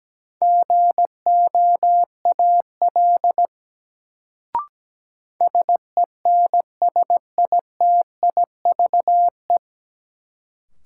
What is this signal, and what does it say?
Morse code, heard as a keyed tone, about 17 wpm, 700 Hz. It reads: GOAL SENSITIVE
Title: Sentences from Top 1400 Words - New Word Drill Alt - 17wpm